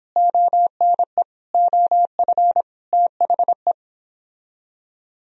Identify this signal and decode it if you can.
Morse code, heard as a keyed tone, about 31 wpm, 700 Hz. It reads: ONE OF THE